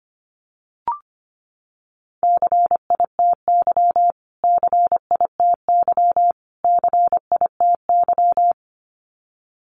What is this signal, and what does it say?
Morse code, heard as a keyed tone, about 24 wpm, 700 Hz. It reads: CITY CITY CITY
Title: Top 500 Words Repeated 25wpm